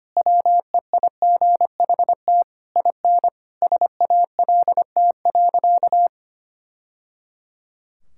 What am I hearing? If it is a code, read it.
Morse code, heard as a keyed tone, about 25 wpm, 700 Hz. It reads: WEIGHT IN SALT.
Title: Sentences from Top 1000 Words - Review Part 5 - 25wpm